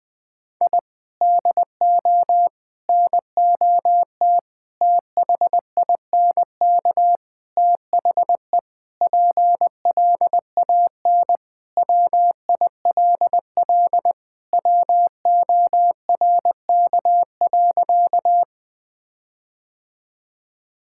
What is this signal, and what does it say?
Morse code, heard as a keyed tone, about 20 wpm, 700 Hz. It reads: I DO NOT THINK THE PLAN WILL WORK.
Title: Sentences with 5-Word Phrases - 20wpm